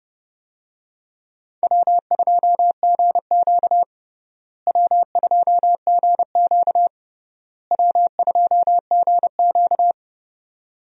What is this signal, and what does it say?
Morse code, heard as a keyed tone, about 30 wpm, 700 Hz. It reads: W2GQ W2GQ W2GQ
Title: USA Call Signs - 4 Characters - Repeated 3x - 30wpm